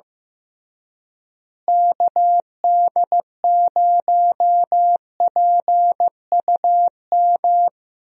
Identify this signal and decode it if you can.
Morse code, heard as a keyed tone, about 15 wpm, 700 Hz. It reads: KD0PUM
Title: USA Call Signs - 6 Characters - Repeated 3x - 15wpm